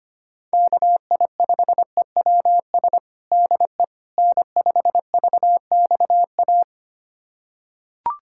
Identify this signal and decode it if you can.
Morse code, heard as a keyed tone, about 25 wpm, 700 Hz. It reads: KI5EWS DE N5VXA
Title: USA Callsign DE Callsign 25wpm